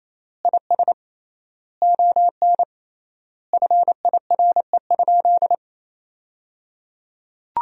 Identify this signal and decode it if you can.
Morse code, heard as a keyed tone, about 28 wpm, 700 Hz. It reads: IS ON FIRE?
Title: 3x Word-Spacing - Sentences from Top 400 Words - 28wpm